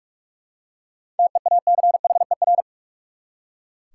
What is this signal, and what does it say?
Morse code, heard as a keyed tone, about 42 wpm, 700 Hz. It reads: TEACHER
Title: Seven Letter Words - 45wpm